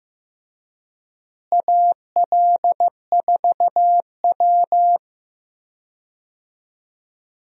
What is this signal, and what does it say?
Morse code, heard as a keyed tone, about 15 wpm, 700 Hz. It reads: AL4W
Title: USA Call Signs - 4 Characters - Repeated 2x - 15wpm